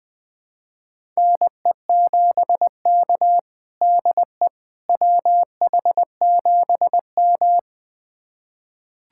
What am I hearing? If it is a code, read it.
Morse code, heard as a keyed tone, about 20 wpm, 700 Hz. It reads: NE7K DE WH7M
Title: USA Callsign DE Callsign - 4 Characters - 20wpm